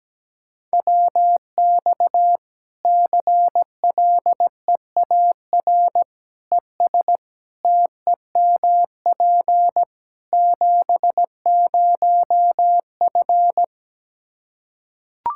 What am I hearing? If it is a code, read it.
Morse code, heard as a keyed tone, about 17 wpm, 700 Hz. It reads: WX CLEAR ES TEMP 70F
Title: Practice QSO - Exchange 2 - Info - 17wpm